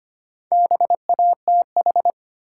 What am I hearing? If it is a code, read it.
Morse code, heard as a keyed tone, about 25 wpm, 700 Hz. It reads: BATH
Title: Sentences from Top 1600 Words - New Word Drill - 25wpm